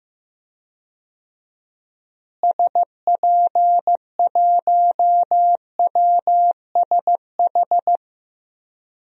Morse code, heard as a keyed tone, about 15 wpm, 700 Hz. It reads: SP1WSH
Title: DXCC-100 Call Signs - 6 Characters - Repeated 3x - 15wpm